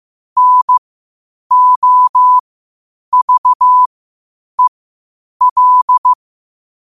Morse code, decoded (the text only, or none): NOVEL